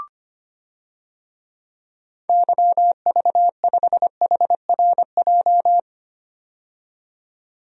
Morse code, heard as a keyed tone, about 25 wpm, 700 Hz. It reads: YV5HRJ
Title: DXCC-100 Call Signs - 6 Characters - 25wpm